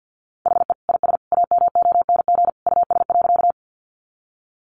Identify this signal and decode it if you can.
Morse code, heard as a keyed tone, about 50 wpm, 700 Hz. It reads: HE IS AMONG US.
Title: Sentences from Top 500 Words 50wpm